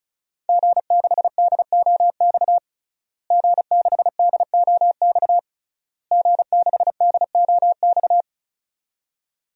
Morse code, heard as a keyed tone, about 34 wpm, 700 Hz. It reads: G6DOX G6DOX G6DOX
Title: DXCC-100 Call Signs - 5 Characters - Repeated 3x - 35wpm